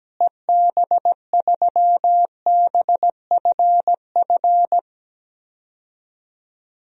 Morse code, heard as a keyed tone, about 17 wpm, 700 Hz. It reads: EB3BFF
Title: DXCC-100 Call Signs - 6 Characters - Repeated 3x - 17wpm